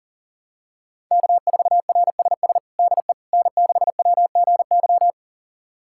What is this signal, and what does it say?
Morse code, heard as a keyed tone, about 37 wpm, 700 Hz. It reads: K4RSS DE N6WGY
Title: USA Callsign DE Callsign 40wpm